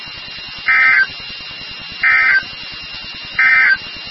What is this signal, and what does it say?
Emergency Alert System (SAME) tones, heard as harsh buzzing bursts. It is heard from 0.6 to 3.8 s.
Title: NOAA Weather Radio WXK36